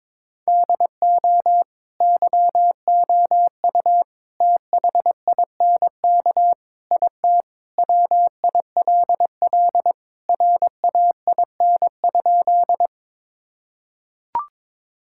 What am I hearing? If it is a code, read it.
Morse code, heard as a keyed tone, about 22 wpm, 700 Hz. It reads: DO YOU THINK IT WILL RAIN?
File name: practice / 22wpm / Sentences with 4-Word Phrases - 22wpm